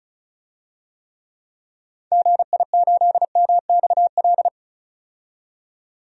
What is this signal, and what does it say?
Morse code, heard as a keyed tone, about 36 wpm, 700 Hz. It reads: GI8MXL
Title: DXCC-100 Call Signs - 6 Characters - 35wpm